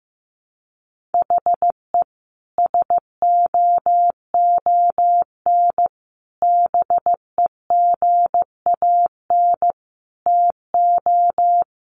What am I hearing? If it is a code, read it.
Morse code, heard as a keyed tone, about 14 wpm, 700 Hz. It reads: HE SOON BEGAN TO
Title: Sentences from Top 1100 Words 15wpm